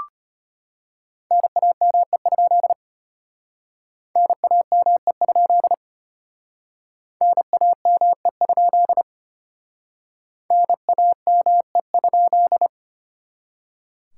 Morse code, heard as a keyed tone, about 32 wpm, 700 Hz. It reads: NAME? NAME? NAME? NAME?